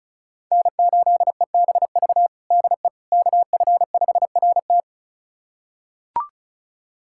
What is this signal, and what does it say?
Morse code, heard as a keyed tone, about 36 wpm, 700 Hz. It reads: N8EBV DE KF5RT
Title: USA Callsign DE Callsign - 5 Characters - 35wpm